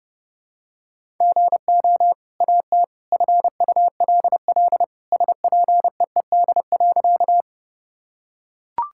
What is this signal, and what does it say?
Morse code, heard as a keyed tone, about 32 wpm, 700 Hz. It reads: GO AT FULL SPEED.
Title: Sentences from Top 1000 Words - Review Part 5 - 30wpm